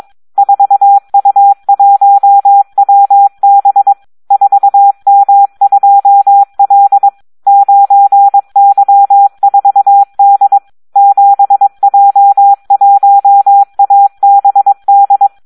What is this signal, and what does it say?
Morse code, heard as a keyed tone, about 22 wpm, 800 Hz. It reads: 4U1WB 4M2L 9Y4D 7J1ABD
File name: Audio practica CW / SD 1 22